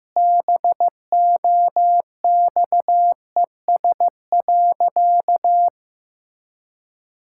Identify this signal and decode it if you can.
Morse code, heard as a keyed tone, about 15 wpm, 700 Hz. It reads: BOXES.
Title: Pangram Challenge v2 - 15wpm